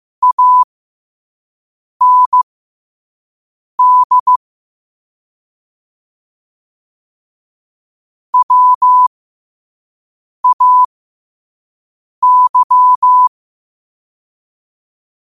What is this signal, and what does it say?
Morse code, heard as a keyed tone, about 15 wpm, spaced as for 5 wpm, 1000 Hz. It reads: AND WAY
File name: qotdfiles / 5wpm